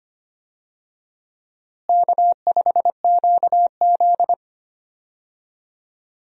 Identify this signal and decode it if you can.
Morse code, heard as a keyed tone, about 25 wpm, 700 Hz. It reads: K5QZ
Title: USA Call Signs - 4 Characters - Repeated 2x - 25wpm